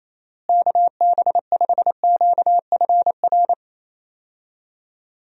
Morse code, heard as a keyed tone, about 28 wpm, 700 Hz. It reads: KB5QFR
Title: USA Call Signs - 6 Characters - Repeated 3x - 28wpm